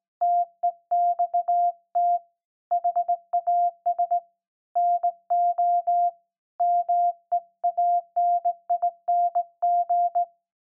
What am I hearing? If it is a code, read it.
Morse code, heard as a keyed tone, about 19 wpm, 700 Hz. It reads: TEXT HAS NO MEANING
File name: se / pre2 / Morse Code01-1(Fast)